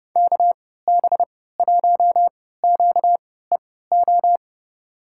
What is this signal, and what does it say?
Morse code, heard as a keyed tone, about 30 wpm, spaced as for 17 wpm, 700 Hz. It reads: KB1QEO
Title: USA Call Signs - 6 Characters - 3x Character Spacing - 30wpm